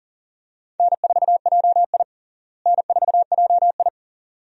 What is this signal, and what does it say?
Morse code, heard as a keyed tone, about 40 wpm, 700 Hz. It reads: N4JI N4JI
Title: USA Call Signs - 4 Characters - Repeated 2x - 40wpm